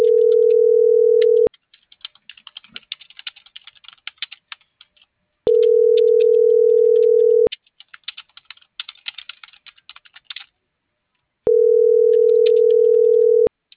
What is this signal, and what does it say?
Ringback tone (North America), 440 + 480 Hz.